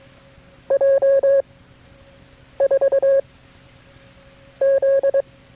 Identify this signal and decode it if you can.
Morse code, heard as a keyed tone, about 22 wpm, 540 Hz. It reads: J 4 Z
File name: KB6MT-tape3-side1